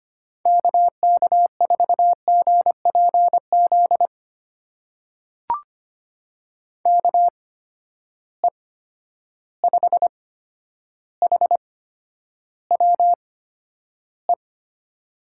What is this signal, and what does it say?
Morse code, heard as a keyed tone, about 25 wpm, spaced as for 11 wpm, 700 Hz. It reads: KK4GPZ KE5HWE
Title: USA Call Signs - 6 Characters - 8x Character Spacing - 25wpm